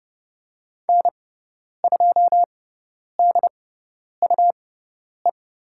Morse code, heard as a keyed tone, about 30 wpm, spaced as for 10 wpm, 700 Hz. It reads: N2DUE
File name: Lesson 18g - Letter m - Call Signs 30wpm